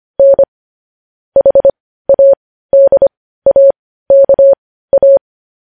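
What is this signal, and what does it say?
Morse code, heard as a keyed tone, about 25 wpm, spaced as for 15 wpm, 550 Hz. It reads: N HADAKA
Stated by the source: Morse code news headlines practice at 15.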